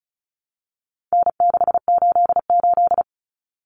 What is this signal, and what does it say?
Morse code, heard as a keyed tone, about 34 wpm, 700 Hz. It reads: N688
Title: Common Cut Numbers - 4-Digit 35wpm